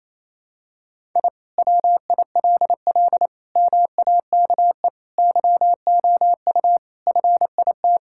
Morse code, heard as a keyed tone, about 28 wpm, 700 Hz. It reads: I WILL MAKE YOU FIT
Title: Sentences from Top 1000 Words - Review Part 5 - 28wpm